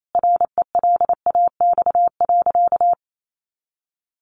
Morse code, heard as a keyed tone, about 28 wpm, 700 Hz. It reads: RELAX.
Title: Sentences from Top 1400 Words 28wpm